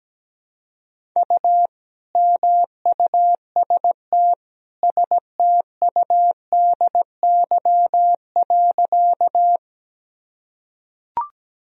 Morse code, heard as a keyed tone, about 17 wpm, 700 Hz. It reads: U MUST STUDY.
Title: Sentences from Top 300 Words and Abbreviations 17wpm